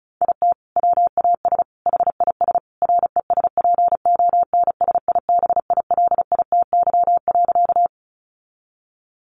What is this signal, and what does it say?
Morse code, heard as a keyed tone, about 34 wpm, 700 Hz. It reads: IT WAS HIS RESPONSIBILITY.
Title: Sentences from Top 1200 Words 35wpm